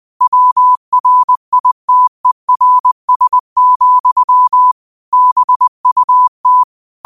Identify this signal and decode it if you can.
Morse code, heard as a keyed tone, about 20 wpm, 1000 Hz. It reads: WRITERS, BUT
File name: qotdfiles / 20wpm